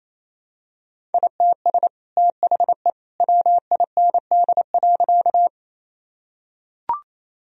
Morse code, heard as a keyed tone, about 28 wpm, 700 Hz. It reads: ITS THE WIND.
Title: Sentences from Top 1000 Words - Review Part 2 - 28wpm